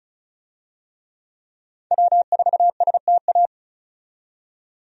Morse code, heard as a keyed tone, about 34 wpm, 700 Hz. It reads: W4STA
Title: USA Call Signs - 5 Characters - 35wpm